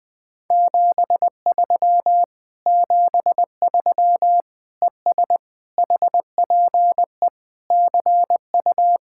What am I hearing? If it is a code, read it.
Morse code, heard as a keyed tone, about 20 wpm, 700 Hz. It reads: 73 73 ES HPE CU
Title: Speed-Racing Common QSO Phrases - 17wpm